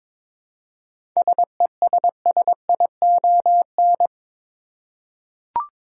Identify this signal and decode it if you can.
Morse code, heard as a keyed tone, about 22 wpm, 700 Hz. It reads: SESSION
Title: Seven Letter Words - 22wpm